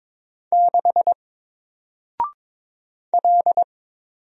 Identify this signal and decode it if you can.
Morse code, heard as a keyed tone, about 22 wpm, 700 Hz. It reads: 6 L
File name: Single-Letter-Number-22wpm